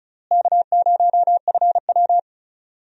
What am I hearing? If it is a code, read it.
Morse code, heard as a keyed tone, about 36 wpm, 700 Hz. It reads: K0FW
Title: USA Call Signs - 4 Characters - Repeated 3x - 35wpm